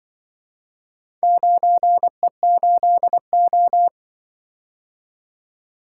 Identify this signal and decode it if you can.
Morse code, heard as a keyed tone, about 23 wpm, 700 Hz. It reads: 9E8O
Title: Speed-Racing - Four Letter-Number Combinations - 20wpm